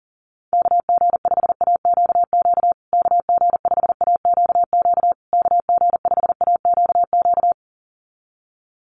Morse code, heard as a keyed tone, about 40 wpm, 700 Hz. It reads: KG5AQQ KG5AQQ KG5AQQ
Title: USA Call Signs Repeated 3x 40wpm